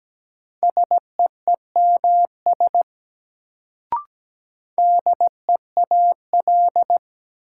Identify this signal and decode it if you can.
Morse code, heard as a keyed tone, about 17 wpm, 700 Hz. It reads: SEEMS DEAL
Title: Sets of 3 Words - New Word Drill - 17wpm